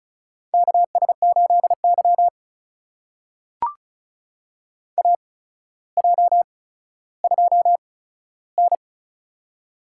Morse code, heard as a keyed tone, about 36 wpm, spaced as for 15 wpm, 700 Hz. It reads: KS8Y AJ2N
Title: USA Call Signs - 4 Characters - 8x Character Spacing - 35wpm